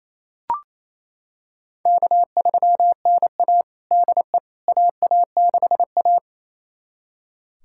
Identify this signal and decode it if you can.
Morse code, heard as a keyed tone, about 28 wpm, 700 Hz. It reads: K3NA DE AA6A
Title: USA Callsign DE Callsign - 4 Characters - 28wpm